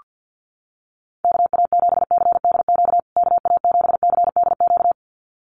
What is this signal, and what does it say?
Morse code, heard as a keyed tone, about 48 wpm, 700 Hz. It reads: KA7CDQ KA7CDQ
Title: USA Call Signs Repeated 2x 50wpm